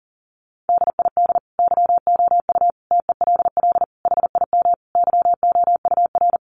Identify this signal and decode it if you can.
Morse code, heard as a keyed tone, about 40 wpm, 700 Hz. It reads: DID YOU TELL HIM YOUR
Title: Sentences from Top 400 Words 40wpm